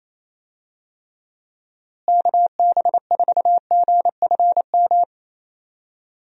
Morse code, heard as a keyed tone, about 28 wpm, 700 Hz. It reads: KB4GFM
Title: USA Call Signs - 6 Characters - Repeated 2x - 28wpm